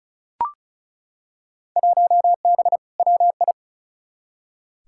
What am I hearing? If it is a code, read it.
Morse code, heard as a keyed tone, about 34 wpm, 700 Hz. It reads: 1B WI
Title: Field Day Exchanges - 35wpm